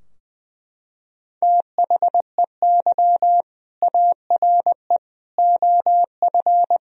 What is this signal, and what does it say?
Morse code, heard as a keyed tone, about 20 wpm, 700 Hz. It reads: THEY ARE OF
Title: Sentences from Top 1000 Words - Review Part 4 - 20wpm